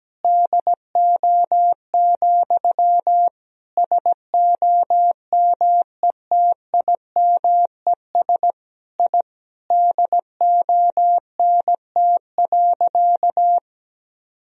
Morse code, heard as a keyed tone, about 17 wpm, 700 Hz. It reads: DO, SOMETIMES I DONT.
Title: Sentences from Top 1000 Words - Review Part 3 - 17wpm